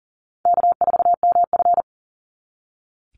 Morse code, heard as a keyed tone, about 40 wpm, 700 Hz. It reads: K4MF
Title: NA SOTA Chaser Callsigns 40wpm